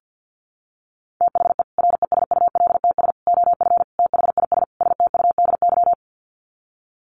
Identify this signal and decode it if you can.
Morse code, heard as a keyed tone, about 50 wpm, 700 Hz. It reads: THE RESULTS OF THIS STUDY
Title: Sets of 5 Words 50wpm